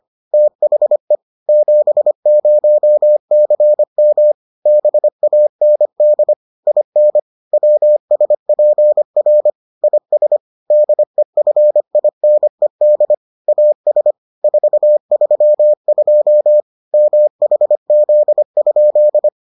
Morse code, heard as a keyed tone, about 25 wpm, 600 Hz. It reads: THE 70CM BAND IN WSPR IS DEFINED AS 432 MHZ?